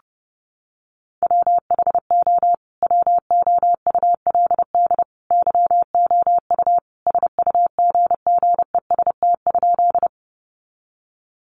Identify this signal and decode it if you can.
Morse code, heard as a keyed tone, about 28 wpm, 700 Hz. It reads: WHO WOULD YOU SUGGEST?
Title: Sentences from Top 1300 Words 30wpm